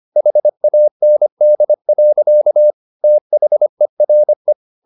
Morse code, heard as a keyed tone, about 25 wpm, 600 Hz. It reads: HAND. THERE
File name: the old man and the sea part 1 25wpm